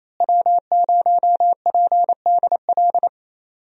Morse code, heard as a keyed tone, about 28 wpm, 700 Hz. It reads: W0PDL
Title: USA Call Signs - 5 Characters - Repeated 3x - 28wpm